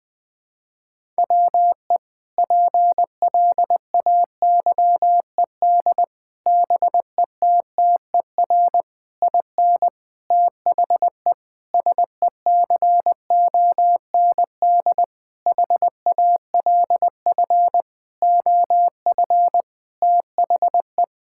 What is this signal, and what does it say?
Morse code, heard as a keyed tone, about 21 wpm, 700 Hz. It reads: WE PLAYED BETTER IN THE SECOND HALF OF THE
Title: Sentences with 5-Word Phrases - Spoken Beginning - 20wpm